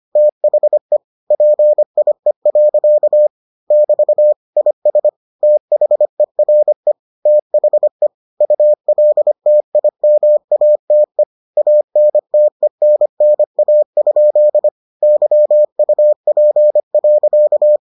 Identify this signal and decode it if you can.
Morse code, heard as a keyed tone, about 25 wpm, 600 Hz. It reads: THE PIE. <BT> IS THERE THE ULTIMATE ANTENNA? YUP.